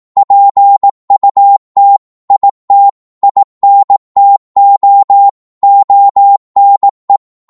Morse code, heard as a keyed tone, about 18 wpm, 800 Hz. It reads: PUT IT INTO ONE